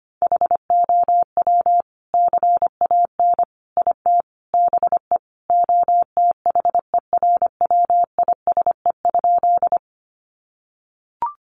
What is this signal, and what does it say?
Morse code, heard as a keyed tone, about 25 wpm, 700 Hz. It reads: HOW CAN IT BE OTHERWISE?
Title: Sentences from Top 1200 Words 25wpm